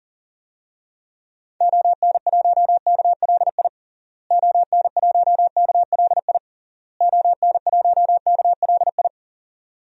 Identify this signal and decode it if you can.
Morse code, heard as a keyed tone, about 40 wpm, 700 Hz. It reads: ON1KLI ON1KLI ON1KLI
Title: DXCC-100 Call Signs - 6 Characters - Repeated 3x - 40wpm